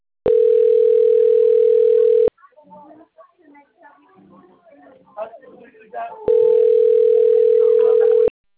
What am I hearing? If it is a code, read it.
Ringback tone (North America), 440 + 480 Hz, with speech.
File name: rg-1600-unknown- - - .54338